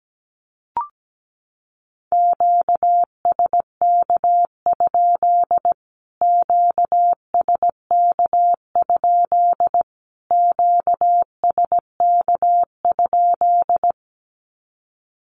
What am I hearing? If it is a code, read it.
Morse code, heard as a keyed tone, about 17 wpm, 700 Hz. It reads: QSK? QSK? QSK?
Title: Q-Codes Repeated 3x 17wpm